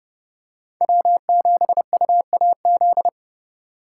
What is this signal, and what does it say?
Morse code, heard as a keyed tone, about 28 wpm, 700 Hz. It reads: W7UAZ